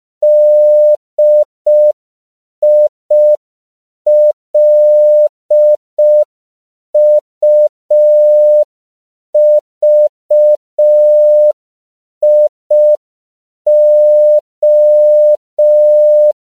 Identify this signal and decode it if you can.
Morse code, heard as a keyed tone, about 5 wpm, 600 Hz. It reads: DILUVIO